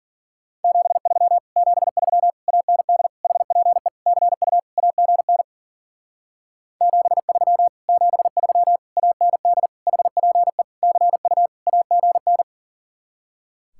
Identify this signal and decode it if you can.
Morse code, heard as a keyed tone, about 50 wpm, 700 Hz. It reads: 73 73 AND HPE CU AGN 73 73 AND HPE CU AGN
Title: Speed-Racing Common QSO Phrases - 40wpm